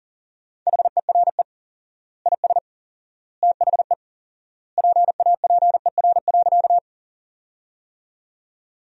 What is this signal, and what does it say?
Morse code, heard as a keyed tone, about 40 wpm, 700 Hz. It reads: HERE IS THE PAPER.
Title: 4x Word-Spacing - Sentences from Top 300 Words - 40wpm